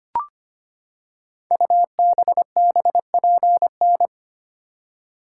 Morse code, heard as a keyed tone, about 25 wpm, 700 Hz. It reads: UBBPN
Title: Code Groups - 5 Letter:Numbers - 25wpm